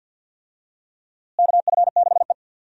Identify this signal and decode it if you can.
Morse code, heard as a keyed tone, about 50 wpm, 700 Hz. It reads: KF6E
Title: USA Call Signs - 4 Characters - Repeated 2x - 50wpm